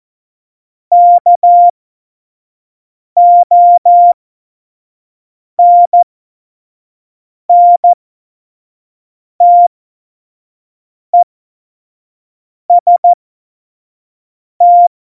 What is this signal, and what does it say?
Morse code, heard as a keyed tone, about 14 wpm, spaced as for 5 wpm, 700 Hz. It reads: KONNTEST
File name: Morsekurs / Witze / Witz 30. April 2019, 25 F70